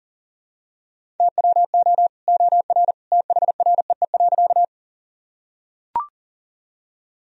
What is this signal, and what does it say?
Morse code, heard as a keyed tone, about 40 wpm, 700 Hz. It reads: TWO OR THREE.
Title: Sentences from Top 1000 Words - Review Part 1 - 40wpm